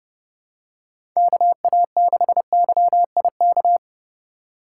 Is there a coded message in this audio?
Yes: Morse code from 1.2 to 3.8 s, heard as a keyed tone, about 31 wpm, 700 Hz. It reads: KA6YIK